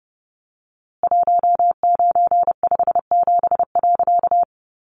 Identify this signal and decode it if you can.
Morse code, heard as a keyed tone, about 30 wpm, 700 Hz. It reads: 1957.
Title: A Sound Of Thunder 30wpm